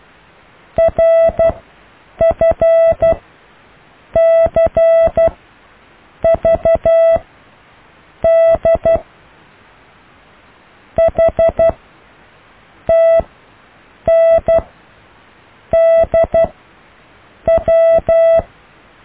Morse code, heard as a keyed tone, about 12 wpm, spaced as for 6 wpm, 660 Hz. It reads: RFCVD HTNDW
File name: cw / IJLUM+CDRO+EKTPV+BFHNW